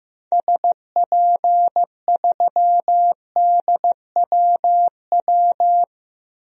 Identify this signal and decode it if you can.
Morse code, heard as a keyed tone, about 15 wpm, 700 Hz. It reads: SP3DWW
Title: DXCC-100 Call Signs - 6 Characters - 15wpm